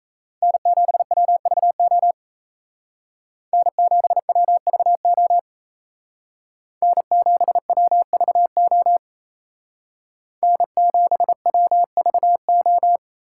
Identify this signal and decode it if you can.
Morse code, heard as a keyed tone, about 36 wpm, 700 Hz. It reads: N7WVO N7WVO N7WVO N7WVO